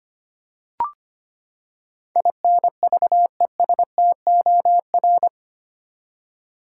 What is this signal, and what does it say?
Morse code, heard as a keyed tone, about 25 wpm, 700 Hz. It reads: INVESTOR
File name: Eight Letter Words - 25wpm